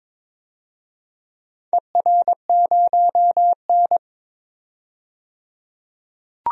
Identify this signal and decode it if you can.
Morse code, heard as a keyed tone, about 21 wpm, 700 Hz. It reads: ER0N
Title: DXCC-100 Call Signs - 4 Characters - 22wpm